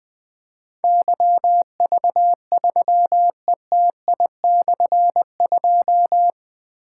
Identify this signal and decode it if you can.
Morse code, heard as a keyed tone, about 20 wpm, 700 Hz. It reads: YV3ETI/2
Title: Difficult Call Signs 20wpm